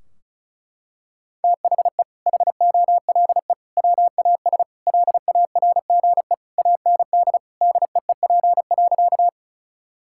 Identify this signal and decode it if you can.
Morse code, heard as a keyed tone, about 36 wpm, 700 Hz. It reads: THE HOLE WAS LARGE AND DEEP.
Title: Sentences from Top 1600 Words 35wpm